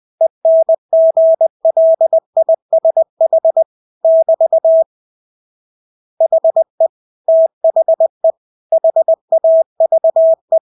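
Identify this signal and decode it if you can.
Morse code, heard as a keyed tone, about 20 wpm, 650 Hz. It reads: ENGLISH <BT> HE THE HAVE
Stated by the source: Morse practice of English words at 20 wpm.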